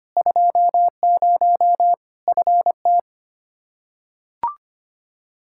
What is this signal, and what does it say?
Morse code, heard as a keyed tone, about 25 wpm, 700 Hz. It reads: 20 FT
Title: K1USN SST QSO Practice 25wpm